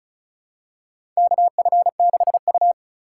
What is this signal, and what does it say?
Morse code, heard as a keyed tone, about 34 wpm, 700 Hz. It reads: KF6U
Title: USA Call Signs - 4 Characters - 4x Character Spacing - 35wpm